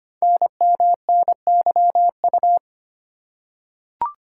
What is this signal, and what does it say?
Morse code, heard as a keyed tone, about 25 wpm, 700 Hz. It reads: NMNYU
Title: Code Groups - 5 Letters - 25wpm